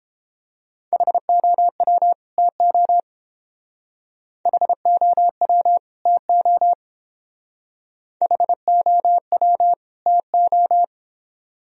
Morse code, heard as a keyed tone, about 28 wpm, 700 Hz. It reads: HOW TO HOW TO HOW TO